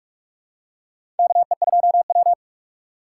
Morse code, heard as a keyed tone, about 46 wpm, 700 Hz. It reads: KE2W
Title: USA Call Signs - 4 Characters - 6x Character Spacing - 45wpm